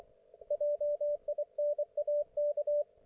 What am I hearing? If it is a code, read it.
Morse code, heard as a keyed tone, about 24 wpm, 590 Hz, with radio noise. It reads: JINAK
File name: Archiv / CUC212 / ok1fhi cul212